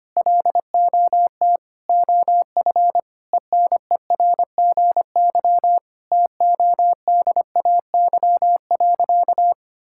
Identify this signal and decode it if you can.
Morse code, heard as a keyed tone, about 25 wpm, 700 Hz. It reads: LOT OF ENERGY TODAY.
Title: Sentences from Top 1000 Words - Review Part 6 - 25wpm